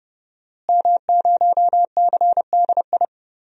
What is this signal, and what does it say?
Morse code, heard as a keyed tone, about 30 wpm, 700 Hz. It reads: M0CDI
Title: DXCC-100 Call Signs - 5 Characters - Repeated 3x - 30wpm